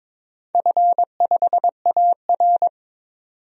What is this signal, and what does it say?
Morse code, heard as a keyed tone, about 22 wpm, 700 Hz. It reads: F5AR
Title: DXCC-100 Call Signs - 4 Characters - 22wpm